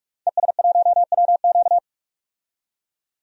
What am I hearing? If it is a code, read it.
Morse code, heard as a keyed tone, about 46 wpm, 700 Hz. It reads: ES1WQ
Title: DXCC-100 Call Signs - 5 Characters - 45wpm